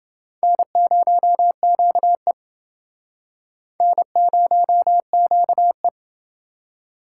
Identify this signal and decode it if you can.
Morse code, heard as a keyed tone, about 28 wpm, 700 Hz. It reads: N0QE N0QE